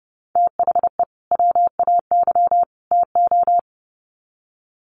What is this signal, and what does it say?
Morse code, heard as a keyed tone, about 30 wpm, 700 Hz. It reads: THE WAY TO
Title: Sets of 3 Words 30wpm